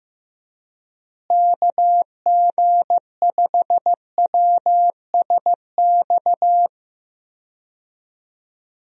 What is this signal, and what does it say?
Morse code, heard as a keyed tone, about 15 wpm, 700 Hz. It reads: KG5WSX
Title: USA Call Signs - 6 Characters - Repeated 3x - 15wpm